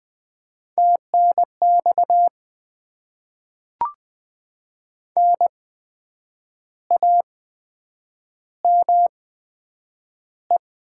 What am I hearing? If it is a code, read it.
Morse code, heard as a keyed tone, about 20 wpm, spaced as for 8 wpm, 700 Hz. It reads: TNX NAME
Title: Top 20 QSO Elements - 8x Character Spacing - 20wpm